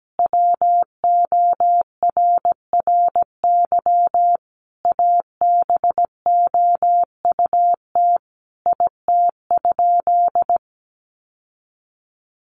Morse code, heard as a keyed tone, about 17 wpm, 700 Hz. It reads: WORRY ABOUT IT?
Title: Sentences from Top 1100 Words 17wpm